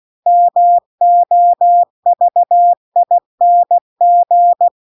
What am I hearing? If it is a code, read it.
Morse code, heard as a keyed tone, about 16 wpm, 700 Hz. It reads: MOVING
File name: CW-The-Big-Twoheaded-River-1-16WPM